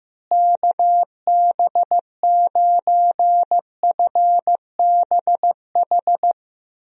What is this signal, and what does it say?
Morse code, heard as a keyed tone, about 15 wpm, 700 Hz. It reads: KB9FBH
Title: USA Call Signs - 6 Characters - Repeated 2x - 15wpm